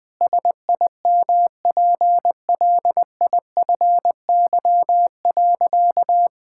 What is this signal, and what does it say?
Morse code, heard as a keyed tone, about 21 wpm, 700 Hz. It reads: SIMPLIFY.
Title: Sentences with 7-Word Phrases - Spoken Beginning - 20wpm